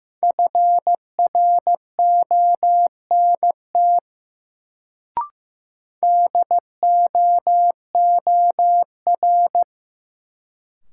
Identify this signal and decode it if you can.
Morse code, heard as a keyed tone, about 15 wpm, 700 Hz. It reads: FRONT DOOR
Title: Sets of 2 Words - New Word Drill - 15wpm